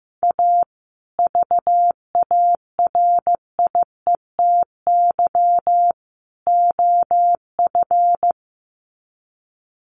Morse code, heard as a keyed tone, about 15 wpm, 700 Hz. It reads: A VARIETY OF
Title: Sets of 3 Words 15wpm